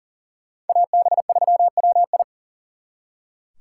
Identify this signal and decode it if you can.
Morse code, heard as a keyed tone, about 52 wpm, 700 Hz. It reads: AB3WI